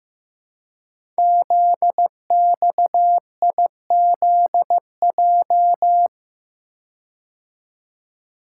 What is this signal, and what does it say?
Morse code, heard as a keyed tone, about 15 wpm, 700 Hz. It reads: ZXIZJ
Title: Code Groups - 5 Letters - 15wpm